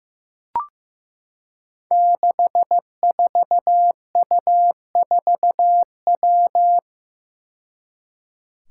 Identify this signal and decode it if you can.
Morse code, heard as a keyed tone, about 15 wpm, 700 Hz. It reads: 64U4W
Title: Five Challenging Letter-Number Combinations 15wpm